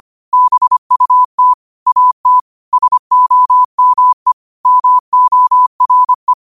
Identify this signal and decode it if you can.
Morse code, heard as a keyed tone, about 25 wpm, 1000 Hz. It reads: BUT AT SOME MORE